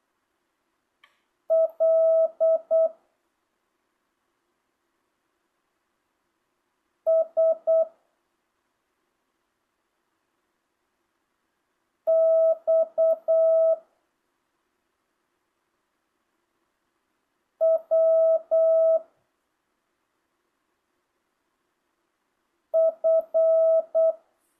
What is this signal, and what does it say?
Morse code, heard as a keyed tone, about 8 wpm, 650 Hz. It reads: L S X W F
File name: jh8cbh / 2025cw